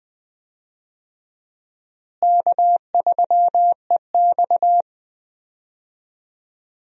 Morse code, heard as a keyed tone, about 20 wpm, 700 Hz. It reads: K3EX
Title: USA Call Signs - 4 Characters - Repeated 3x - 20wpm